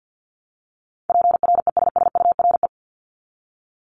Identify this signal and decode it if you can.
Morse code, heard as a keyed tone, about 48 wpm, 700 Hz. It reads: PRESSURE
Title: Top 751-1000 Words 50wpm